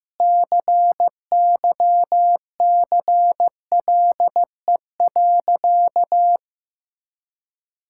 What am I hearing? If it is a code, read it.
Morse code, heard as a keyed tone, about 16 wpm, 700 Hz. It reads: CYCLE.
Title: Sentences from Top 1000 Words - Review Part 6 - 15wpm